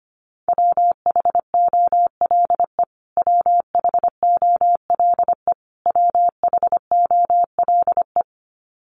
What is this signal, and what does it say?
Morse code, heard as a keyed tone, about 25 wpm, 700 Hz. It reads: WHOLE WHOLE WHOLE
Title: Top 500 Words Repeated 25wpm